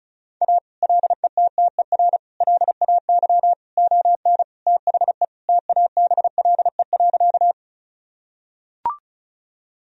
Morse code, heard as a keyed tone, about 36 wpm, 700 Hz. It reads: A LETTER LAY ON THE TABLE.
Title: Sentences from Top 1000 Words - Review Part 2 - 35wpm